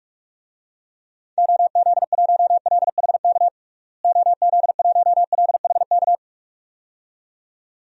Morse code, heard as a keyed tone, about 46 wpm, 700 Hz. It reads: OZ1LHK OZ1LHK
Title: DXCC-100 Call Signs - 6 Characters - Repeated 2x - 45wpm